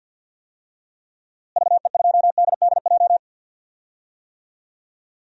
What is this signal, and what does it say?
Morse code, heard as a keyed tone, about 52 wpm, 700 Hz. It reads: VE2DDJ
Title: DXCC-100 Call Signs - 6 Characters - Repeated 3x - 50wpm